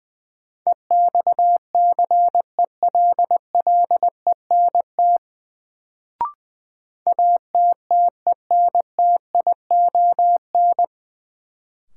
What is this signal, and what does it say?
Morse code, heard as a keyed tone, about 20 wpm, 700 Hz. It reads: EXCELLENT ATTENTION
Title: Sentences from Top 1000 Words - New Word Drill - 20wpm